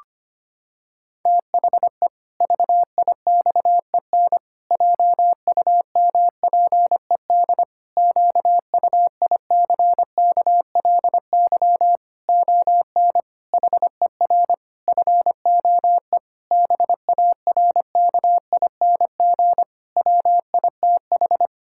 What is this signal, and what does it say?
Morse code, heard as a keyed tone, about 25 wpm, 700 Hz. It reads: THE VIXEN JUMPED QUICKLY ON HER FOE BARKING WITH
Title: Pangram Challenge v2 - 25wpm